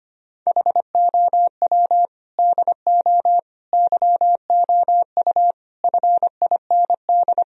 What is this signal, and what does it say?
Morse code, heard as a keyed tone, about 25 wpm, 700 Hz. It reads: HOW DO YOU FIND
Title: Sentences from Top 1000 Words - Review Part 4 - 25wpm